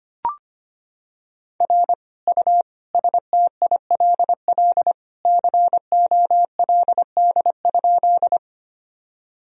Morse code, heard as a keyed tone, about 25 wpm, 700 Hz. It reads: R U STILL COLD?
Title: Sentences from Top 400 Words and Abbreviations 25wpm